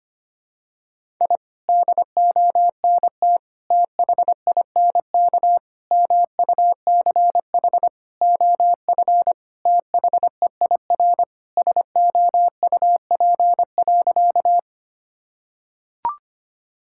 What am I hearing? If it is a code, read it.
Morse code, heard as a keyed tone, about 25 wpm, 700 Hz. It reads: I DONT THINK MUCH OF THEIR SOUP.
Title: Sentences from Top 1000 Words - Review Part 6 - 25wpm